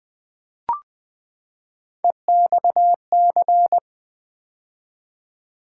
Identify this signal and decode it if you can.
Morse code, heard as a keyed tone, about 20 wpm, 700 Hz. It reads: EXC
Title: Build A Word - 3 Letters At A Time - 9-Letter Words - 20wpm